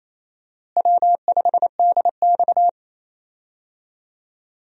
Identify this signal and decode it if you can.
Morse code, heard as a keyed tone, about 28 wpm, 700 Hz. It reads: W5DX
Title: USA Call Signs - 4 Characters - Repeated 2x - 28wpm